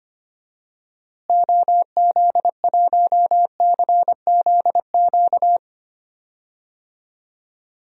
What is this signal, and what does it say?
Morse code, heard as a keyed tone, about 25 wpm, 700 Hz. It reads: OZ1CZQ
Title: DXCC-100 Call Signs - 6 Characters - Repeated 2x - 25wpm